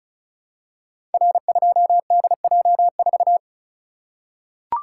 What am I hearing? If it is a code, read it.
Morse code, heard as a keyed tone, about 36 wpm, 700 Hz. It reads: R2DJ4
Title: Code Groups - 5 Letter:Numbers - 35wpm